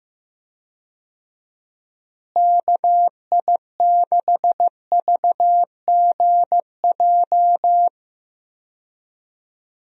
Morse code, heard as a keyed tone, about 15 wpm, 700 Hz. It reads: KI6VGJ
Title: USA Call Signs - 6 Characters - Repeated 2x - 15wpm